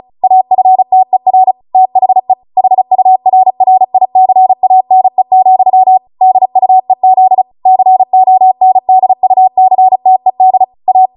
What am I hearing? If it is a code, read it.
Morse code, heard as a keyed tone, about 36 wpm, 750 Hz. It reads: AFTER THE HURRICANE, DUEZ CONDUCTED A